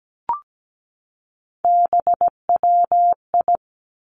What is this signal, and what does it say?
Morse code, heard as a keyed tone, about 17 wpm, 700 Hz. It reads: BWI
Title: Three Challenging Letter-Number Combinations 17wpm